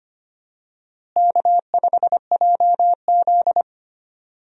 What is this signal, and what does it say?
Morse code, heard as a keyed tone, about 25 wpm, 700 Hz. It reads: K5JZ
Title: USA Call Signs - 4 Characters - Repeated 2x - 25wpm